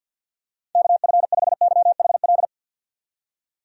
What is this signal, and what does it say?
Morse code, heard as a keyed tone, about 49 wpm, 700 Hz. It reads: KF5YHL
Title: USA Call Signs - 6 Characters - 50wpm